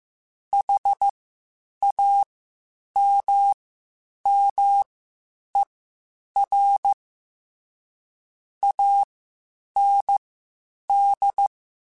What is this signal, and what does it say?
Morse code, heard as a keyed tone, about 15 wpm, spaced as for 8 wpm, 800 Hz. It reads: HAMMER AND
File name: Misc / Misc-mp3s / WordPairs sp @13